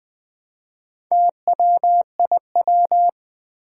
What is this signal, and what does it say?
Morse code, heard as a keyed tone, about 20 wpm, 700 Hz. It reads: TWIW
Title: Four Challenging Letter-Number Combinations 20wpm